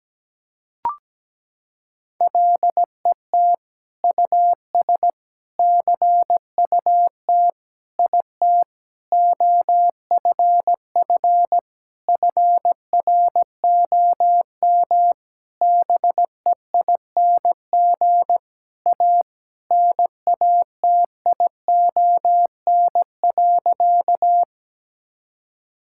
Morse code, heard as a keyed tone, about 17 wpm, 700 Hz. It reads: LET US CUT IT OFF FROM BEING A NATION.
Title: Sentences from Top 1000 Words - Review Part 6 - 17wpm